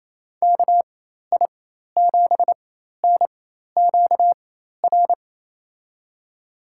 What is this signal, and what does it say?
Morse code, heard as a keyed tone, about 28 wpm, spaced as for 13 wpm, 700 Hz. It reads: KI7NQR
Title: USA Call Signs - 6 Characters - 4x Character Spacing - 28wpm